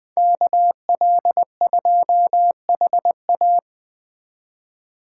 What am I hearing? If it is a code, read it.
Morse code, heard as a keyed tone, about 20 wpm, 700 Hz. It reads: KL2HA